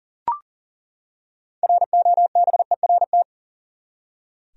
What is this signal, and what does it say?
Morse code, heard as a keyed tone, about 40 wpm, 700 Hz. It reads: ROBERT
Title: USA Common Names - 40wpm